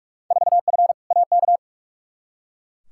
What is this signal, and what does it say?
Morse code, heard as a keyed tone, about 44 wpm, 700 Hz. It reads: 4F AK
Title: Field Day Exchanges Encore - 45wpm